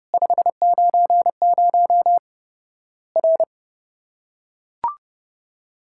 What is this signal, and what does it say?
Morse code, heard as a keyed tone, about 30 wpm, 700 Hz. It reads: 590 R
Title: Worked All Europe DX Contest - QTC Exchange - 30wpm